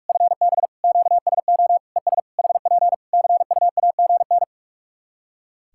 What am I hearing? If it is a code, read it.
Morse code, heard as a keyed tone, about 46 wpm, 700 Hz. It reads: FB QSO ES HP CUAGN
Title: Practice QSO - Exchange 4 - Ending - 45wpm